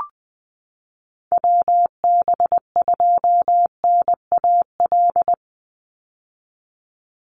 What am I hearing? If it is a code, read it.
Morse code, heard as a keyed tone, about 20 wpm, 700 Hz. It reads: WB2NAL
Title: USA Call Signs 20wpm